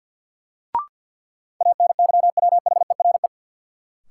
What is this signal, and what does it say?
Morse code, heard as a keyed tone, about 52 wpm, 700 Hz. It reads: ANYWHERE
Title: Sentences from Top 1400 Words - New Word Drill - 50wpm